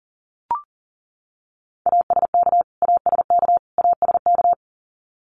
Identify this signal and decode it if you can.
Morse code, heard as a keyed tone, about 40 wpm, 700 Hz. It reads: ASK ASK ASK
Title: Top 200 Words Repeated 40wpm